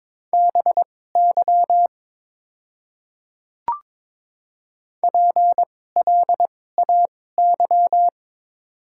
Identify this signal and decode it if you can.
Morse code, heard as a keyed tone, about 22 wpm, spaced as for 16 wpm, 700 Hz. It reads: BY PLAY
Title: 2x Character Spacing - Top 200 Words - 22wpm